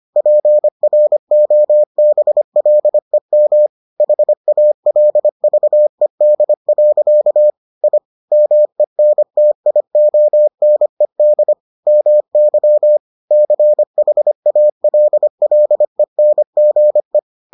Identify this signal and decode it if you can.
Morse code, heard as a keyed tone, about 25 wpm, 600 Hz. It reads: PROBLEM HALVED. I MENTIONED MY CHALLENGE